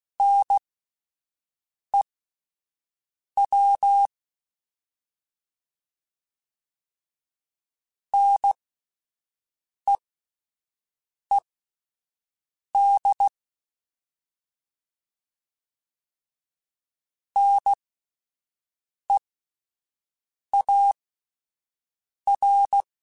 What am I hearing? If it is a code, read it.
Morse code, heard as a keyed tone, about 16 wpm, spaced as for 5 wpm, 800 Hz. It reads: NEW NEED NEAR